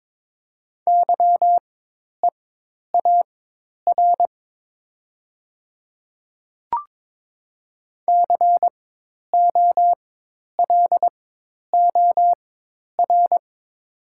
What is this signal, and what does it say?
Morse code, heard as a keyed tone, about 22 wpm, spaced as for 10 wpm, 700 Hz. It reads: YEAR COLOR
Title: 4x Character Spacing - Top 300 Words - 22wpm